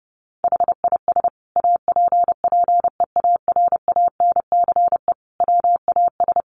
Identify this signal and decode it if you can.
Morse code, heard as a keyed tone, about 30 wpm, 700 Hz. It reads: HIS APPEARANCE WAS
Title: Sentences from Top 1100 Words 30wpm